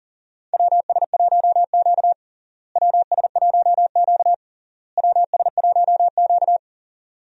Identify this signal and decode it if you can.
Morse code, heard as a keyed tone, about 37 wpm, 700 Hz. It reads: WS1Q WS1Q WS1Q
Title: USA Call Signs - 4 Characters - Repeated 3x - 40wpm